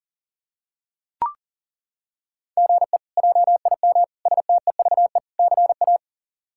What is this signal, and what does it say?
Morse code, heard as a keyed tone, about 37 wpm, 700 Hz. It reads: GE JIM STEVE CA
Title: K1USN SST QSO Practice - 40wpm